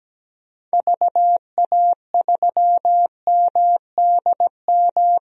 Morse code, heard as a keyed tone, about 17 wpm, 700 Hz. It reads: VA3MDM
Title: DXCC-100 Call Signs - 6 Characters - 17wpm